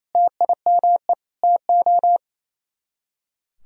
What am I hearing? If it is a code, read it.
Morse code, heard as a keyed tone, about 34 wpm, 700 Hz. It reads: TIME TO
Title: Speed-Racing - Sets of 4 Words 28wpm